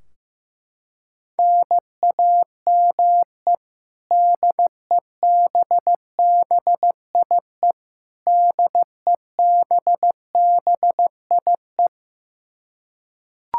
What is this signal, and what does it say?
Morse code, heard as a keyed tone, about 15 wpm, 700 Hz. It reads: NAME DEBBIE DEBBIE
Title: Practice QSO - Exchange 1 - Beginning - 15wpm